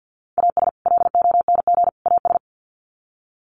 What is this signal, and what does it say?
Morse code, heard as a keyed tone, about 50 wpm, 700 Hz. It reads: AS LONG AS
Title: Sets of 3 Words 50wpm